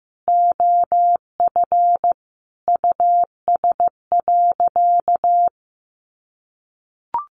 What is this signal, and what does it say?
Morse code, heard as a keyed tone, about 15 wpm, 700 Hz. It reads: OF US.
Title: Sentences from Top 1100 Words 15wpm